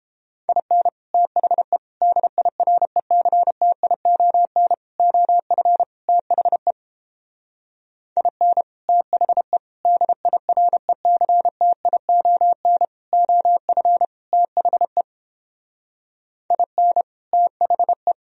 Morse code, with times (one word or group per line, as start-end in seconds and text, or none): 0.5-0.9 IN
1.1-1.8 THE
2.0-4.7 DIRECTION
5.0-5.8 OF
6.1-6.7 THE
8.2-8.6 IN
8.9-9.6 THE
9.9-12.8 DIRECTION
13.1-14.0 OF
14.3-15.0 THE
16.5-17.0 IN
17.3-18.1 THE